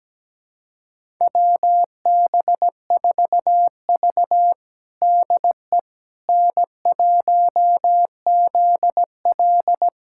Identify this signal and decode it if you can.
Morse code, heard as a keyed tone, about 17 wpm, 700 Hz. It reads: WB4V DE N1ZL
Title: USA Callsign DE Callsign - 4 Characters - 17wpm